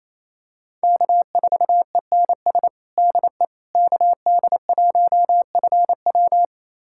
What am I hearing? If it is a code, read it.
Morse code, heard as a keyed tone, about 28 wpm, 700 Hz. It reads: K4ENS DE KD1FW
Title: USA Callsign DE Callsign 28wpm